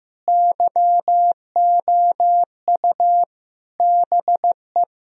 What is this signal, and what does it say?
Morse code, heard as a keyed tone, about 15 wpm, 700 Hz. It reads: YOU BE
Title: Sentences from Top 1000 Words - Review Part 3 - 15wpm